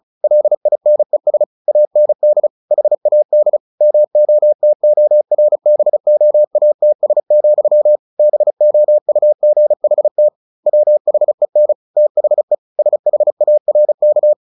Morse code, heard as a keyed tone, about 34 wpm, 600 Hz. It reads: LINES AND HAD MOTORBOATS, BOUGHT WHEN THE SHARK
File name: the old man and the sea / the old man and the sea 1 35wpm